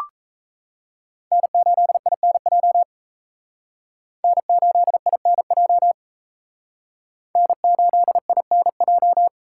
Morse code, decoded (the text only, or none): N8INJ N8INJ N8INJ